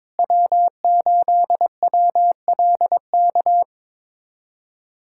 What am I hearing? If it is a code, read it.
Morse code, heard as a keyed tone, about 22 wpm, 700 Hz. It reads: W8WLK
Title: USA Callsign DE Callsign 22wpm